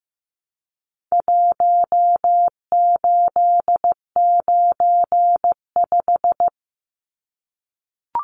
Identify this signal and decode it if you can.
Morse code, heard as a keyed tone, about 15 wpm, 700 Hz. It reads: 1895
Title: Numbers - Contest Style - 4-Digit 15wpm